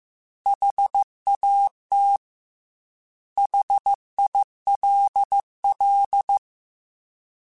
Morse code, heard as a keyed tone, about 15 wpm, 800 Hz. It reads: HAT HILL
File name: Alphabetical / Alphabetical mp3s / Hs @13